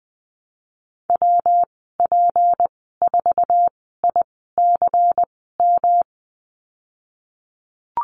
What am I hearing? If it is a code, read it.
Morse code, heard as a keyed tone, about 19 wpm, spaced as for 14 wpm, 700 Hz. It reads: WP4ICM
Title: 154 Lesson 25g - Letter p - Call Signs 20wpm